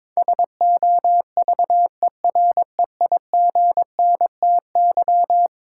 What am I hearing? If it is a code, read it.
Morse code, heard as a keyed tone, about 22 wpm, 700 Hz. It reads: SOVEREIGNTY
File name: practice / 22wpm / Eleven Letter Words - 22wpm